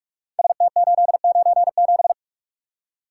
Morse code, heard as a keyed tone, about 44 wpm, 700 Hz. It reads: ST897
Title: Code Groups - 5 Letter:Numbers - 45wpm